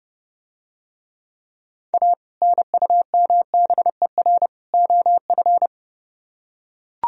Morse code, heard as a keyed tone, about 31 wpm, 700 Hz. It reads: A NUMBER OF